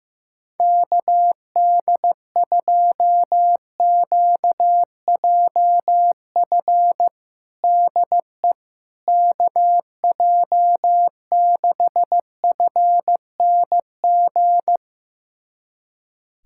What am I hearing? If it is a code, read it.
Morse code, heard as a keyed tone, about 15 wpm, 700 Hz. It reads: KD2QJF DE KJ6FNG
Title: USA Callsign DE Callsign - 6 Characters - 15wpm